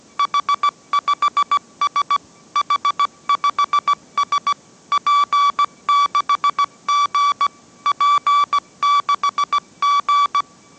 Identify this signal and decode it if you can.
Morse code, heard as a keyed tone, about 17 wpm, 1150 Hz. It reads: H5S H5S P6G P6G
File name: mrx / mrx080 18wpm